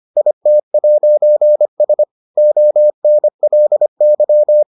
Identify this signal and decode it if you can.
Morse code, heard as a keyed tone, about 25 wpm, 600 Hz. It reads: IT'S ONLY